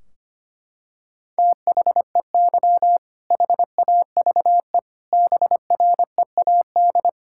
Morse code, heard as a keyed tone, about 25 wpm, 700 Hz. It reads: THEY HAVE BREAD
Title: Sentences from Top 1600 Words 25wpm